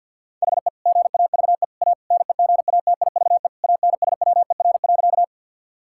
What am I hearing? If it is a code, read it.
Morse code, heard as a keyed tone, about 50 wpm, 700 Hz. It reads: HE GAVE A NEGATIVE ANSWER.
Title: Sentences from Top 1000 Words - Review Part 6 - 50wpm